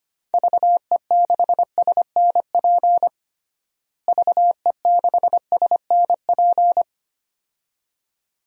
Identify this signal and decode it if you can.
Morse code, heard as a keyed tone, about 25 wpm, 700 Hz. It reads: VE6SNP VE6SNP
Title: DXCC-100 Call Signs - 6 Characters - Repeated 2x - 25wpm